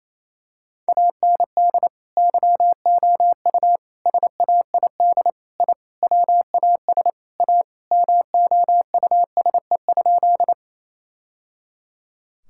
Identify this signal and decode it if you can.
Morse code, heard as a keyed tone, about 28 wpm, 700 Hz. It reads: AND YOU SAID I WAS A MOUSE?
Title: Sentences from Top 1000 Words - Review Part 5 - 28wpm